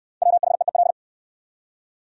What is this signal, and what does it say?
Morse code, heard as a keyed tone, about 68 wpm, 700 Hz. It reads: WHEEL